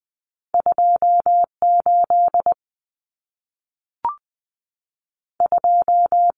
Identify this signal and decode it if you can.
Morse code, heard as a keyed tone, about 20 wpm, 700 Hz. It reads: 28 2
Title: Common Cut Numbers - 2-Digit 20wpm